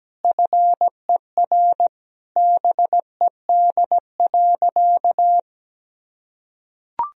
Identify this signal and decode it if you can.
Morse code, heard as a keyed tone, about 17 wpm, 700 Hz. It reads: FER BED.
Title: Sentences from Top 500 Words and Abbreviations 17wpm